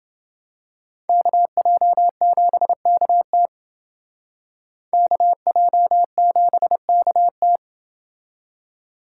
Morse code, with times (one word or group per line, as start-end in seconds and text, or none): 1.1-3.5 KJ7KT
4.9-7.6 KJ7KT